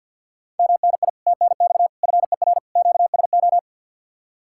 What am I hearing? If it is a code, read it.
Morse code, heard as a keyed tone, about 60 wpm, 700 Hz. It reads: MNI TNX FER QSO
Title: Speed-Racing Common QSO Phrases Encore - 50wpm